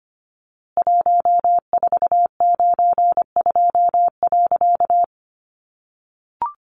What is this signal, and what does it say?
Morse code, heard as a keyed tone, about 25 wpm, 700 Hz. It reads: 1492.
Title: A Sound Of Thunder 25wpm